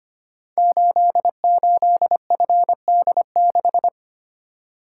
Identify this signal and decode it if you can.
Morse code, heard as a keyed tone, about 25 wpm, 700 Hz. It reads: 88FD6
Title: Code Groups - 5 Letter:Numbers - 25wpm